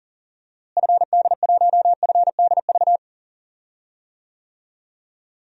Morse code, heard as a keyed tone, about 40 wpm, 700 Hz. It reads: FD1FDV
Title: DXCC-100 Call Signs - 6 Characters - Repeated 3x - 40wpm